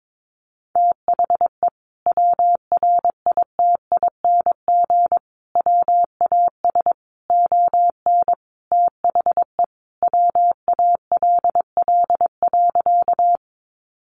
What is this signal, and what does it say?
Morse code, heard as a keyed tone, about 22 wpm, 700 Hz. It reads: THE WRITING WAS ON THE WALL.
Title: Sentences from Top 1100 Words 22wpm